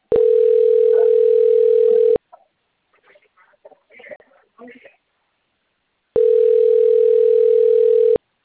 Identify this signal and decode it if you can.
Ringback tone (North America), 440 + 480 Hz.